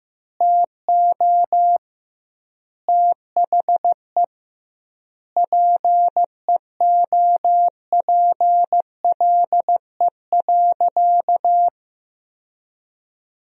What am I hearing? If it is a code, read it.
Morse code, heard as a keyed tone, about 16 wpm, 700 Hz. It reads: TO THE PEOPLE.
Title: 2x Word-Spacing - Sentences from Top 400 Words - 15wpm